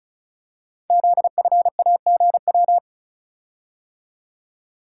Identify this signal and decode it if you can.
Morse code, heard as a keyed tone, about 36 wpm, 700 Hz. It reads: ZFAGW
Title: Code Groups - 5 Letters - 35wpm